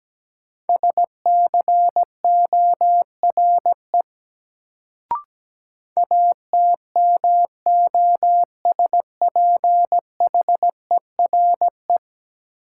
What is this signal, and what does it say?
Morse code, heard as a keyed tone, about 17 wpm, 700 Hz. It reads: SCORE ATMOSPHERE
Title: Sentences from Top 1600 Words - New Word Drill - 17wpm